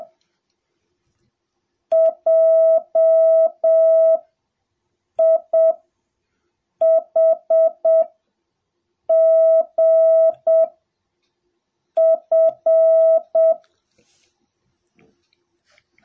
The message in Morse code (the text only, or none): JIHGF